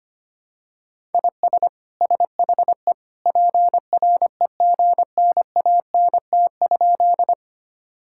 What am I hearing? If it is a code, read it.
Morse code, heard as a keyed tone, about 25 wpm, 700 Hz. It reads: IS SHE PREGNANT?
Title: Sentences from Top 1000 Words - Review Part 6 - 25wpm